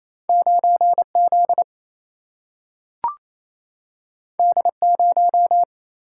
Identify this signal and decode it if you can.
Morse code, heard as a keyed tone, about 28 wpm, 700 Hz. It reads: 9Z D0
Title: Single-Letter Single-Number 28wpm